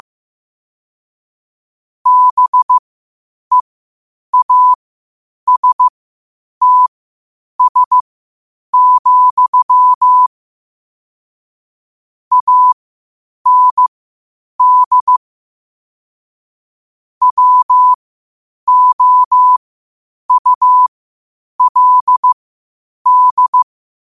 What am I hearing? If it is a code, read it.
Morse code, heard as a keyed tone, about 15 wpm, spaced as for 8 wpm, 1000 Hz. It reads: BEASTS, AND WOULD